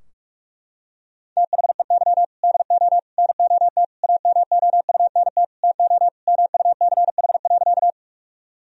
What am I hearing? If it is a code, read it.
Morse code, heard as a keyed tone, about 47 wpm, 700 Hz. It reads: THEY DO NOT AMOUNT TO MUCH.
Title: Sentences from Top 1000 Words - Review Part 4 - 45wpm